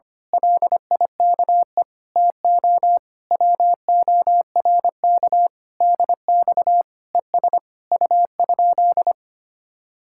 Morse code, heard as a keyed tone, about 25 wpm, 700 Hz. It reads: LIKE TO WORK DX ES U?
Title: Speed-Racing Common QSO Phrases - 25wpm